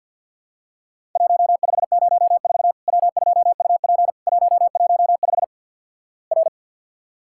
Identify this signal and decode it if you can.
Morse code, heard as a keyed tone, about 53 wpm, 700 Hz. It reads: 1504 W2UP 115 R
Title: Worked All Europe DX Contest - QTC Exchange with Repeats - 50wpm